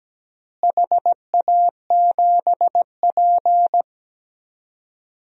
Morse code, heard as a keyed tone, about 17 wpm, 700 Hz. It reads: HA7P
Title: DXCC-100 Call Signs - 4 Characters - Repeated 3x - 17wpm